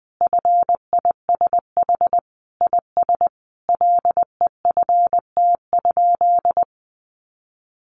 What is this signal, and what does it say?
Morse code, heard as a keyed tone, about 20 wpm, 700 Hz. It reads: FISH IS LEFT?
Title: Sentences from Top 1300 Words 20wpm